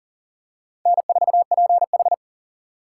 Morse code, heard as a keyed tone, about 40 wpm, 700 Hz. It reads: N4PH
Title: USA Call Signs - 6x Character Spacing - 40wpm